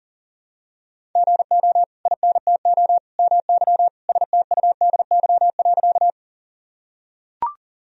Morse code, heard as a keyed tone, about 40 wpm, 700 Hz. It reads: GO INTO MY STUDY.
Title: Sentences from Top 1000 Words - Review Part 1 - 40wpm